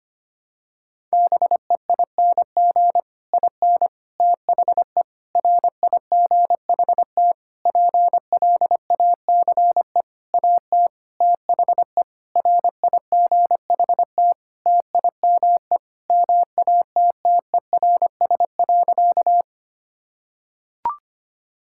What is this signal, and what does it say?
Morse code, heard as a keyed tone, about 25 wpm, 700 Hz. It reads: BEING IN THE RIGHT PLACE AT THE RIGHT TIME MATTERS.
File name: practice / 25wpm / Sentences with 9-Word Phrases - 25wpm